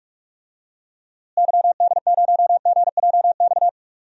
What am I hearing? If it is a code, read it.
Morse code, heard as a keyed tone, about 45 wpm, 700 Hz. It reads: YD0GJX
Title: DXCC-100 Call Signs - 6 Characters - Repeated 2x - 45wpm